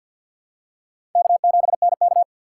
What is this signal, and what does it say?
Morse code, heard as a keyed tone, about 51 wpm, 700 Hz. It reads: K7NK
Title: USA Call Signs - 4 Characters - 2x Character Spacing - 50wpm